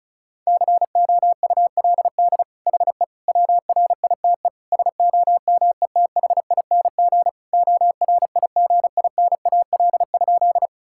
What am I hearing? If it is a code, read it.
Morse code, heard as a keyed tone, about 36 wpm, 700 Hz. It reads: COULD HE WRITE SOMETHING ORIGINAL?
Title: Sentences from Top 1000 Words - Review Part 6 - 35wpm